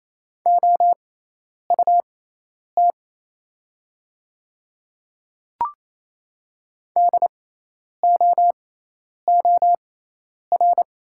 Morse code, heard as a keyed tone, about 28 wpm, spaced as for 10 wpm, 700 Hz. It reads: OUT DOOR
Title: 6x Character Spacing - Top 500 Words - 28wpm